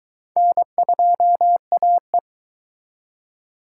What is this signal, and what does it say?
Morse code, heard as a keyed tone, about 24 wpm, 700 Hz. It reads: N2AE